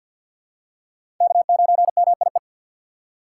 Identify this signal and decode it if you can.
Morse code, heard as a keyed tone, about 46 wpm, 700 Hz. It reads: K9MIE
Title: USA Call Signs - 5 Characters - 4x Character Spacing - 50wpm